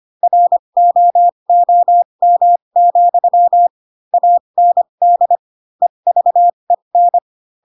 Morse code, heard as a keyed tone, about 25 wpm, spaced as for 22 wpm, 700 Hz. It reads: ROOM, AND EVEN